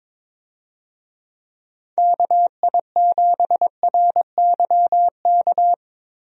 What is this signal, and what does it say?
Morse code, heard as a keyed tone, about 22 wpm, 700 Hz. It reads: KI7RYK
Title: USA Call Signs - 6 Characters - Repeated 2x - 22wpm